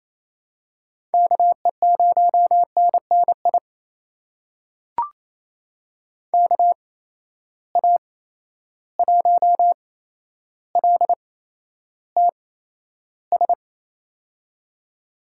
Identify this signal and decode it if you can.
Morse code, heard as a keyed tone, about 29 wpm, spaced as for 12 wpm, 700 Hz. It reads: KE0NNI KA1LTS
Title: USA Call Signs - 6 Characters - 8x Character Spacing - 28wpm